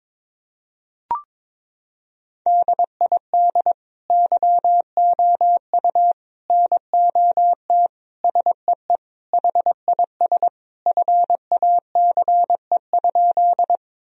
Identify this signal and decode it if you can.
Morse code, heard as a keyed tone, about 22 wpm, 700 Hz. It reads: DID YOU NOT SEE HIS FACE?
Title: Sentences from Top 1000 Words - Review Part 2 - 22wpm